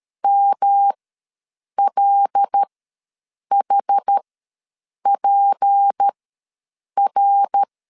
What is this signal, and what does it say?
Morse code, heard as a keyed tone, about 13 wpm, spaced as for 7 wpm, 800 Hz. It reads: MLHPR
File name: LEZIONI / 6 PKX0 / 6 PKX ws 12wpm 3106